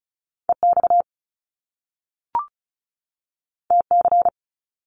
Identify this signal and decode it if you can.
Morse code, heard as a keyed tone, about 36 wpm, 700 Hz. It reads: EX TC
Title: Top Two Letter Combinations in Words 35wpm